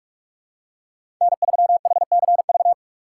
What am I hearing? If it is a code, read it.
Morse code, heard as a keyed tone, about 46 wpm, 700 Hz. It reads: N3HCV
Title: USA Call Signs - 5 Characters - 3x Character Spacing - 45wpm